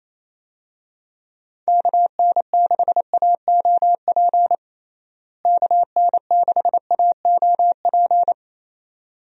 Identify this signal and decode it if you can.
Morse code, heard as a keyed tone, about 28 wpm, 700 Hz. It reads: KN6AOP KN6AOP
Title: USA Call Signs - 6 Characters - Repeated 2x - 28wpm